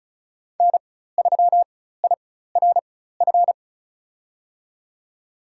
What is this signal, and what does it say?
Morse code, heard as a keyed tone, about 34 wpm, spaced as for 16 wpm, 700 Hz. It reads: N3IRF
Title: USA Call Signs - 4x Character Spacing - 35wpm